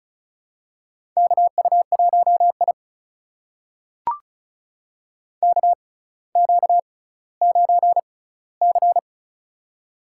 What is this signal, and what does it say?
Morse code, heard as a keyed tone, about 36 wpm, spaced as for 18 wpm, 700 Hz. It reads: KU1I KQ9C
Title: USA Call Signs - 4 Characters - 6x Character Spacing - 35wpm